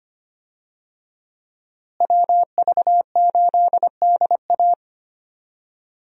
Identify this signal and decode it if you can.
Morse code, heard as a keyed tone, about 25 wpm, 700 Hz. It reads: WV8DA
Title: USA Call Signs - 5 Characters - Repeated 3x - 25wpm